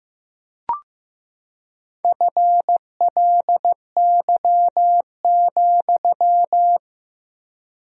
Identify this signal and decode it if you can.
Morse code, heard as a keyed tone, about 14 wpm, 700 Hz. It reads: FLY,
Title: 3x Word-Spacing - Sentences from Top 500 Words - 15wpm